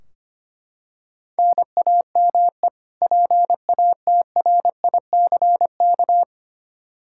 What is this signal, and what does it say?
Morse code, heard as a keyed tone, about 25 wpm, 700 Hz. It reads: NAME PATRICK
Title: K1USN SST QSO Practice 25wpm